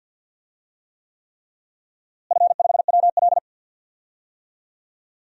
Morse code, heard as a keyed tone, about 51 wpm, 700 Hz. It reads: F5WL
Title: DXCC-100 Call Signs - 4 Characters - Repeated 3x - 50wpm